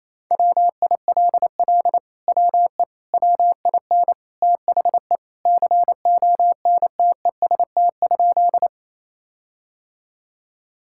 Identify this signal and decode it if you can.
Morse code, heard as a keyed tone, about 28 wpm, 700 Hz. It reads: WILL WE WIN THE CONTEST?
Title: Sentences from Top 1600 Words 28wpm